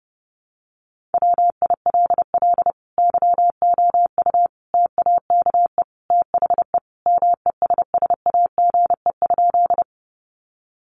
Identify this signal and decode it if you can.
Morse code, heard as a keyed tone, about 30 wpm, 700 Hz. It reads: WILL YOU TAKE THE MESSAGE?
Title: Sentences from Top 1100 Words 30wpm